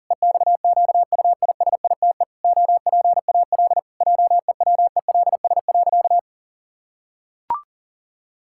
Morse code, heard as a keyed tone, about 40 wpm, 700 Hz. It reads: EXQUISITE OPAL JEWELS.
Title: Pangram Challenge v2 - 40wpm